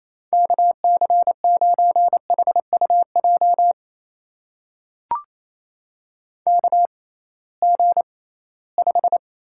Morse code, heard as a keyed tone, about 28 wpm, spaced as for 18 wpm, 700 Hz. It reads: KC9HUJ KG5
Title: USA Call Signs - 6x Character Spacing - 28wpm